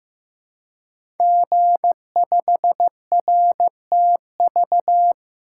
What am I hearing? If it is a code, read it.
Morse code, heard as a keyed tone, about 15 wpm, 700 Hz. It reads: G5RTV
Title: Code Groups - 5 Letter:Numbers - 15wpm